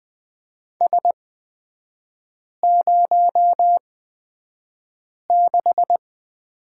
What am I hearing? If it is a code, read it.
Morse code, heard as a keyed tone, about 20 wpm, spaced as for 5 wpm, 700 Hz. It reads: S06
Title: 229 Lesson 37e - Letter x - Three Characters 20wpm